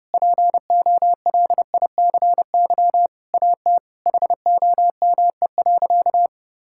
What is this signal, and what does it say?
Morse code, heard as a keyed tone, about 30 wpm, 700 Hz. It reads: POLICY AT HOME.
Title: Sentences from Top 1000 Words - Review Part 5 - 30wpm